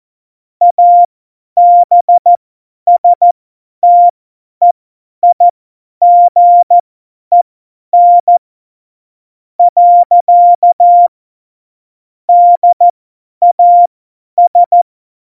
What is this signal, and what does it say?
Morse code, heard as a keyed tone, about 14 wpm, spaced as for 10 wpm, 700 Hz. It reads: ABSTEIGEN . DAS
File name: Morsekurs / Witze / Witz 18 August 2020, 50 F70